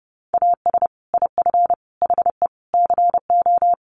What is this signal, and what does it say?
Morse code, heard as a keyed tone, about 30 wpm, 700 Hz. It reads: AS IF HE CO
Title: Sentences from Top 1300 Words 30wpm